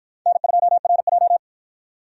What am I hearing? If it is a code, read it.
Morse code, heard as a keyed tone, about 49 wpm, 700 Hz. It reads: N2RJ